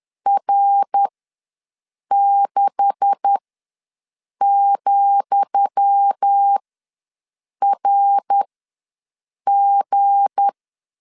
Morse code, heard as a keyed tone, about 10 wpm, spaced as for 6 wpm, 800 Hz. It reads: R6,RG
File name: LEZIONI / 10 SIGNS / 10D COMA ws 10wpm 4314